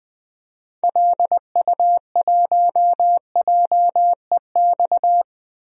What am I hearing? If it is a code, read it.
Morse code, heard as a keyed tone, about 20 wpm, 700 Hz. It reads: LU1JEX
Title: DXCC-100 Call Signs - 6 Characters - 20wpm